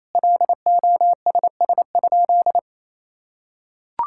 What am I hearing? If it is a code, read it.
Morse code, heard as a keyed tone, about 28 wpm, 700 Hz. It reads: LOSS?
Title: Sentences from Top 1600 Words 28wpm